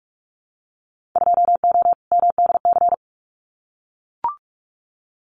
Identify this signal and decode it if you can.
Morse code, heard as a keyed tone, about 44 wpm, 700 Hz. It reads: 2O MDC
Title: Winter Field Day Exchanges 45wpm